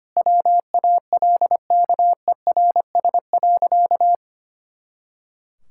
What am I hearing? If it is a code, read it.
Morse code, heard as a keyed tone, about 25 wpm, 700 Hz. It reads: WALKERS.
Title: Pangram Challenge v2 - 25wpm